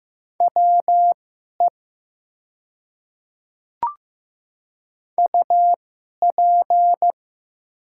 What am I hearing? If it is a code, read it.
Morse code, heard as a keyed tone, about 15 wpm, spaced as for 11 wpm, 700 Hz. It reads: WE UP
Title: 2x Character Spacing - Top 100 Words - 15wpm